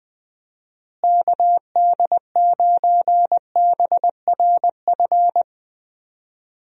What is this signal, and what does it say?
Morse code, heard as a keyed tone, about 20 wpm, 700 Hz. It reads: KD9BRF
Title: USA Call Signs - 6 Characters - Repeated 2x - 20wpm